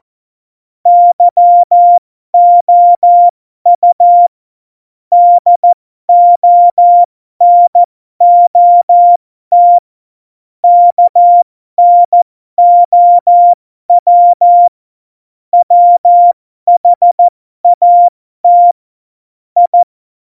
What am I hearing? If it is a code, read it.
Morse code, heard as a keyed tone, about 14 wpm, spaced as for 12 wpm, 700 Hz. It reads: YOU DO NOT KNOW WHAT I
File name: Morsekurs / Witze